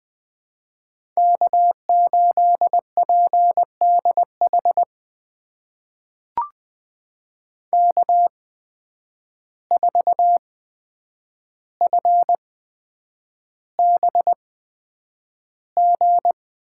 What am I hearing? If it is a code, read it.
Morse code, heard as a keyed tone, about 20 wpm, spaced as for 9 wpm, 700 Hz. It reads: K8PDH K4FBG
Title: USA Call Signs - 5 Characters - 8x Character Spacing - 20wpm